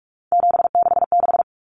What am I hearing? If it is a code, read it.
Morse code, heard as a keyed tone, about 44 wpm, 700 Hz. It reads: 766
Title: Numbers - 3-Digit 45wpm